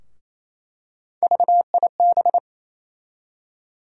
Morse code, heard as a keyed tone, about 28 wpm, 700 Hz. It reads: VIB
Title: USA Call Sign Postfixes - 3 Letters 28wpm